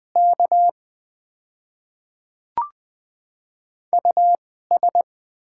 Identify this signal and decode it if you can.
Morse code, heard as a keyed tone, about 21 wpm, spaced as for 15 wpm, 690 Hz. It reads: K US
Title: 2x Character Spacing - Top 300 Words - 20wpm